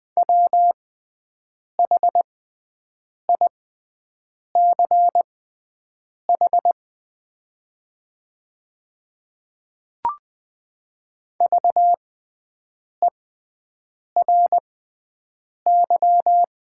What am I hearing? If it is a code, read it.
Morse code, heard as a keyed tone, about 20 wpm, spaced as for 7 wpm, 700 Hz. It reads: WHICH VERY
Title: 6x Character Spacing - Top 200 Words - 20wpm